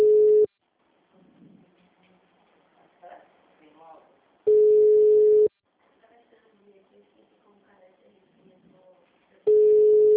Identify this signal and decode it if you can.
Ringback tone (Europe, 425 Hz), with speech.
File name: out- -2006- - - .20230